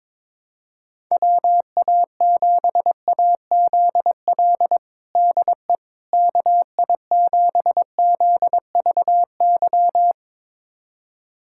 Morse code, heard as a keyed tone, about 21 wpm, 700 Hz. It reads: WA7AZL DE KI7ZVY
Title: USA Callsign DE Callsign - 6 Characters - 22wpm